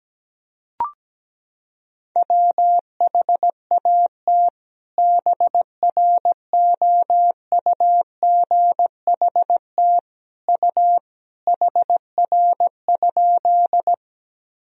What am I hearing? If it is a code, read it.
Morse code, heard as a keyed tone, about 17 wpm, 700 Hz. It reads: WHAT BROUGHT U HR?
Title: Sentences from Top 500 Words and Abbreviations 17wpm